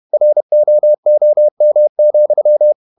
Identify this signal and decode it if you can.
Morse code, heard as a keyed tone, about 31 wpm, 600 Hz. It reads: ROOM,
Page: Never 31wpm | CW med Gnister